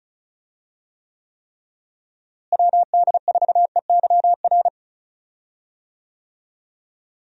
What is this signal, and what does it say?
Morse code, heard as a keyed tone, about 34 wpm, 700 Hz. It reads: WD4EYR
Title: USA Call Signs - 6 Characters - Repeated 3x - 35wpm